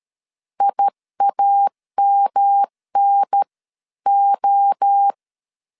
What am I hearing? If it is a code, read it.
Morse code, heard as a keyed tone, about 13 wpm, spaced as for 11 wpm, 800 Hz. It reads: IAMNO
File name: CW / DET / 1 3 1500C S A 12wpm 2905